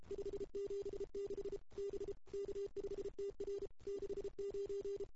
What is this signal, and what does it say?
Morse code, heard as a keyed tone, about 30 wpm, 410 Hz. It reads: 576 B K5TR 69
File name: CW / 2008-ss-cw-k5tr-op-n5rz -02:00